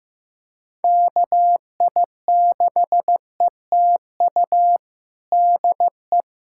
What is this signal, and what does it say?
Morse code, heard as a keyed tone, about 15 wpm, 700 Hz. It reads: KI6ETU DE
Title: USA Callsign DE Callsign - 6 Characters - 15wpm